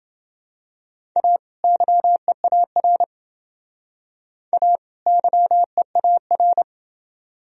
Morse code, heard as a keyed tone, about 28 wpm, 700 Hz. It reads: A YEAR A YEAR